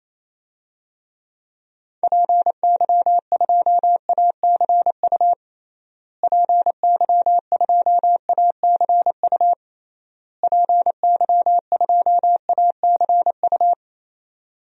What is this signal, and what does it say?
Morse code, heard as a keyed tone, about 28 wpm, 700 Hz. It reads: PY2ACU PY2ACU PY2ACU
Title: DXCC-100 Call Signs - 6 Characters - Repeated 3x - 28wpm